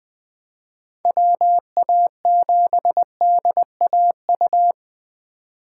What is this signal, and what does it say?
Morse code, heard as a keyed tone, about 20 wpm, 700 Hz. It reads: WA7DAU
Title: Speed Racing - USA 2x3 Call Signs 17wpm